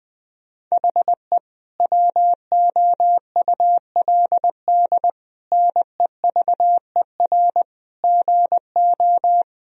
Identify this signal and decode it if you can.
Morse code, heard as a keyed tone, about 20 wpm, 700 Hz. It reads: HE WOULD NEVER GO
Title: Sentences from Top 1000 Words - Review Part 2 - 20wpm